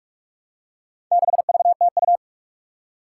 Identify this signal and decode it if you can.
Morse code, heard as a keyed tone, about 42 wpm, 700 Hz. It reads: 6VTU
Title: Four Challenging Letter-Number Combinations 45wpm